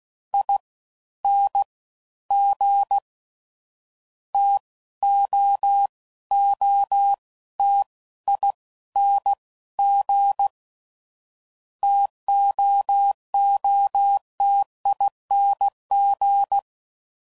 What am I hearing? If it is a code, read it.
Morse code, heard as a keyed tone, about 16 wpm, 800 Hz. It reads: I N G T O O T I N G TOOTING
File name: Combinations / Difficult Combos / T AOEI's